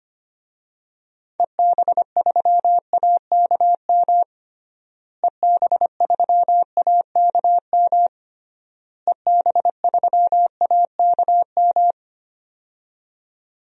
Morse code, heard as a keyed tone, about 25 wpm, 700 Hz. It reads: EB3AKM EB3AKM EB3AKM
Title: DXCC-100 Call Signs - 6 Characters - Repeated 3x - 25wpm